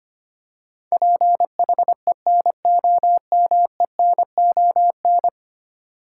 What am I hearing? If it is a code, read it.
Morse code, heard as a keyed tone, about 25 wpm, 700 Hz. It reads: PHENOMENON
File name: Ten Letter Words - 25wpm